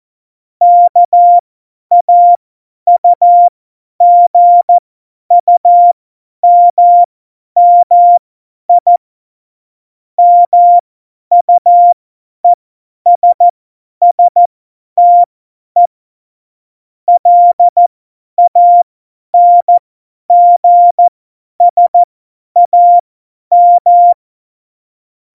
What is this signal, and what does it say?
Morse code, heard as a keyed tone, about 14 wpm, spaced as for 10 wpm, 700 Hz. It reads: KAUGUMMI MUESSTE LANGSAM
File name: Morsekurs / Witze / Witz 02. Januar 2018, 50 F70